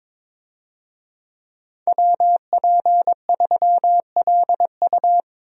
Morse code, heard as a keyed tone, about 22 wpm, 700 Hz. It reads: WP3LU
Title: USA Call Signs - 5 Characters - Repeated 2x - 22wpm